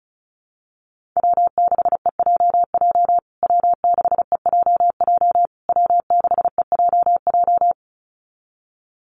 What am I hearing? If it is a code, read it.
Morse code, heard as a keyed tone, about 34 wpm, 700 Hz. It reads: W6EJJ W6EJJ W6EJJ
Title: USA Call Signs Repeated 3x 35wpm